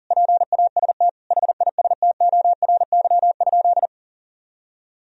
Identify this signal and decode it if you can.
Morse code, heard as a keyed tone, about 40 wpm, 700 Hz. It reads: PAST HISTORY?
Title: Sentences from Top 1000 Words - Review Part 3 - 40wpm